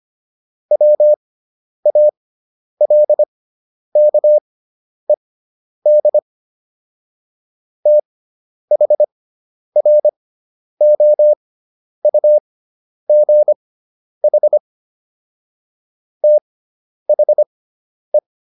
Morse code, heard as a keyed tone, about 25 wpm, spaced as for 10 wpm, 600 Hz. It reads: WALKED THROUGH THE